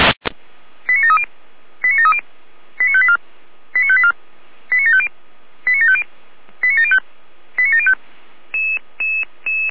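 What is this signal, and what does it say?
Five-tone selective call (ZVEI), heard as quick runs of five tones, tones of about 70 ms. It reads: 89730 89730 89584 89584 89750 89750 89705 89705
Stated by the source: ZVEI five-tone selective call received by SDR.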